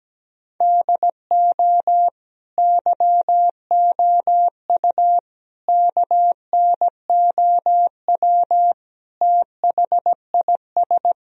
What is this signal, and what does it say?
Morse code, heard as a keyed tone, about 17 wpm, 700 Hz. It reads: DO YOU KNOW THIS
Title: Sentences from Top 1000 Words - Review Part 2 - 17wpm